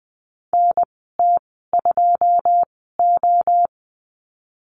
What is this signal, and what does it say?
Morse code, heard as a keyed tone, about 20 wpm, spaced as for 15 wpm, 700 Hz. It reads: NT2O
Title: 122 Lesson 20g - Number 3 - Call Signs 20wpm